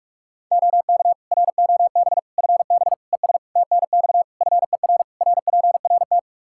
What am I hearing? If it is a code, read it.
Morse code, heard as a keyed tone, about 46 wpm, 700 Hz. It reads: OK ROB FB ES TNX FER RPRT
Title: Practice QSO - Exchange 2 - Info - 45wpm